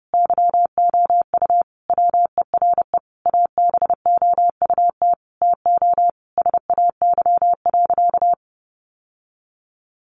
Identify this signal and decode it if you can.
Morse code, heard as a keyed tone, about 30 wpm, 700 Hz. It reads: YOU WERE ABOUT TO SAY.
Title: Sentences from Top 1200 Words 30wpm